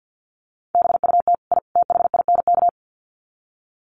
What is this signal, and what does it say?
Morse code, heard as a keyed tone, about 50 wpm, 700 Hz. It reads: BUT I THINK
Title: Sets of 3 Words 50wpm